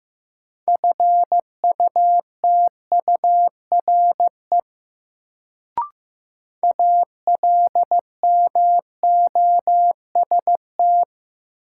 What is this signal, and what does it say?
Morse code, heard as a keyed tone, about 15 wpm, 700 Hz. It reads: FUTURE ALMOST
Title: Sentences from Top 700 Words - New Word Drill - 15wpm